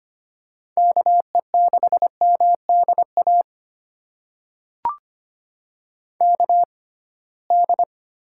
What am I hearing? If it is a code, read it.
Morse code, heard as a keyed tone, about 25 wpm, spaced as for 19 wpm, 700 Hz. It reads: KE6MDA KD
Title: USA Call Signs - 6 Characters - 6x Character Spacing - 25wpm